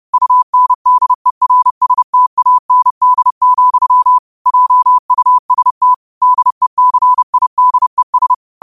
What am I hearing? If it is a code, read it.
Morse code, heard as a keyed tone, about 30 wpm, 1000 Hz. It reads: UNDERSTAND, JUST DECIDES
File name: qotdfiles / 30wpm